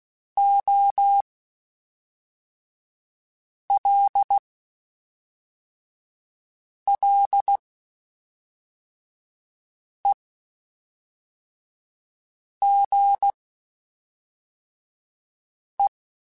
Morse code, heard as a keyed tone, about 16 wpm, spaced as for 3 wpm, 800 Hz. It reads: OLLEGE
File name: QSOs / QSOs-mp3s / WD0BGP 1@5